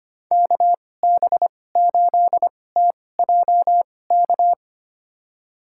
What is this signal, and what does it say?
Morse code, heard as a keyed tone, about 25 wpm, spaced as for 18 wpm, 700 Hz. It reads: KB8TJK
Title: USA Call Signs - 6 Characters - 2x Character Spacing - 25wpm